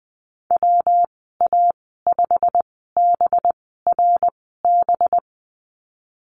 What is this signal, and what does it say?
Morse code, heard as a keyed tone, about 20 wpm, spaced as for 15 wpm, 700 Hz. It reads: WA5BRB
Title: 219 Lesson 35g - Number 0 - Call Signs 20wpm